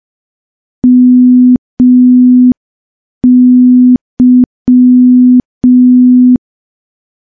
Morse code, heard as a keyed tone, about 5 wpm, 250 Hz. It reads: MY
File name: morse / morse-training-05WPM